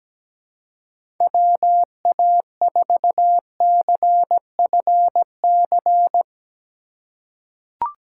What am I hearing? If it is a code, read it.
Morse code, heard as a keyed tone, about 17 wpm, 700 Hz. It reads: WA4CFC
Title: USA Call Signs - 6 Characters - 2x Character Spacing - 17wpm